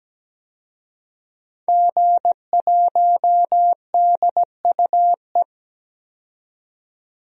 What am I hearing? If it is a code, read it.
Morse code, heard as a keyed tone, about 17 wpm, 700 Hz. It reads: G1DUE
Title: DXCC-100 Call Signs - 5 Characters - 17wpm